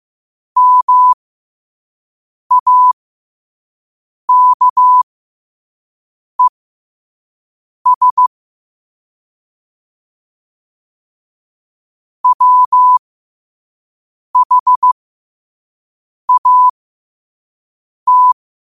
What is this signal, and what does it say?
Morse code, heard as a keyed tone, about 15 wpm, spaced as for 5 wpm, 1000 Hz. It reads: MAKES WHAT
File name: qotdfiles / 5wpm